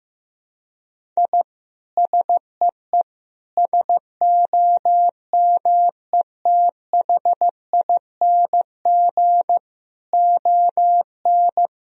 Morse code, heard as a keyed tone, about 15 wpm, 700 Hz. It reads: I SEE SOMETHING ON
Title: Sentences from Top 1000 Words - Review Part 2 - 15wpm